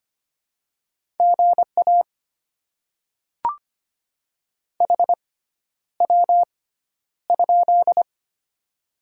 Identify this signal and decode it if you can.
Morse code, heard as a keyed tone, about 25 wpm, spaced as for 11 wpm, 700 Hz. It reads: GA HW?
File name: Top 20 QSO Elements - 6x Character Spacing - 25wpm